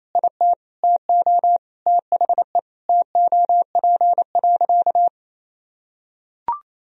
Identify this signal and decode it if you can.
Morse code, heard as a keyed tone, about 28 wpm, 700 Hz. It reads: IT TO THE TOP.
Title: Sentences from Top 1000 Words - Review Part 2 - 28wpm